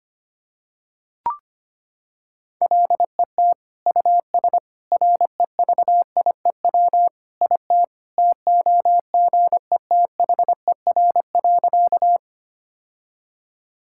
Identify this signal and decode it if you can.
Morse code, heard as a keyed tone, about 25 wpm, 700 Hz. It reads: LET US REVIEW IT TOGETHER.
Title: Sentences from Top 1000 Words - Review Part 6 - 25wpm